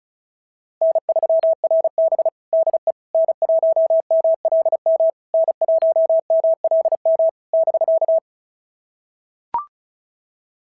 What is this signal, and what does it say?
Morse code, heard as a keyed tone, about 31 wpm, 650 Hz. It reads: N3RB DE N1MLM N1MLM <BK>
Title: SOTA QSO Practice 35wpm